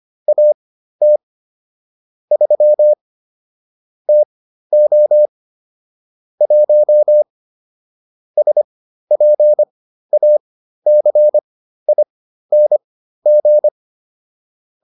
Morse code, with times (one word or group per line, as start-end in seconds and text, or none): 0.3-1.2 AT
2.3-2.9 3
4.1-5.3 TO
6.4-7.2 1
8.4-13.7 SPACING